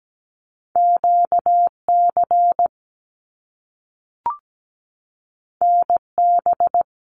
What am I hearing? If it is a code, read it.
Morse code, heard as a keyed tone, about 17 wpm, 700 Hz. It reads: QC NB
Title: Canadian Provinces and Territories 17wpm